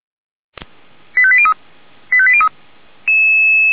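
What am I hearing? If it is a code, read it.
Five-tone selective call (ZVEI), heard as quick runs of five tones, tones of about 70 ms. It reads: 85803 85803
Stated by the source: ZVEI five-tone selective call received on an SDR.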